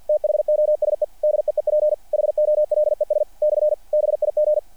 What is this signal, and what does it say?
Morse code, heard as a keyed tone, about 48 wpm, 600 Hz. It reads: THOSE DEEP HOLES / BIG
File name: archived / CRSnet